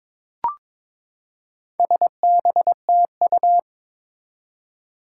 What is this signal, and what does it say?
Morse code, heard as a keyed tone, about 22 wpm, 700 Hz. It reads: SBTU
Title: Four Challenging Letter-Number Combinations 22wpm